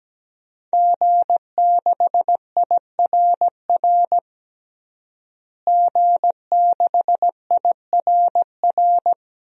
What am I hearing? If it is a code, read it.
Morse code, heard as a keyed tone, about 17 wpm, 700 Hz. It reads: G6IRR G6IRR
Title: DXCC-100 Call Signs - 5 Characters - Repeated 3x - 17wpm